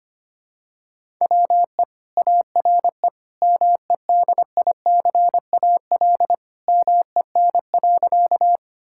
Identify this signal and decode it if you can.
Morse code, heard as a keyed tone, about 25 wpm, 700 Hz. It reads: WE ARE MEDICAL MEN.
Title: Sentences from Top 1000 Words - Review Part 5 - 25wpm